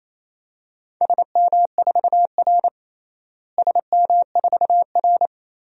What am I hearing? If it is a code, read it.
Morse code, heard as a keyed tone, about 28 wpm, 700 Hz. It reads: SM4R SM4R
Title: DXCC-100 Call Signs - 4 Characters - Repeated 2x - 28wpm